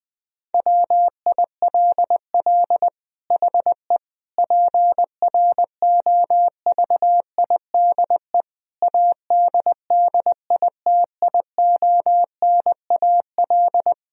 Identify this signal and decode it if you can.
Morse code, heard as a keyed tone, about 20 wpm, 700 Hz. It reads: WILL HE PROVIDE ADDITIONAL
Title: Sentences from Top 1000 Words - Review Part 4 - 20wpm